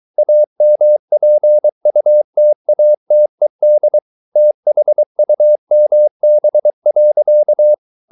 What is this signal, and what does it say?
Morse code, heard as a keyed tone, about 23 wpm, 600 Hz. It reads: AMPUTATED THUMB.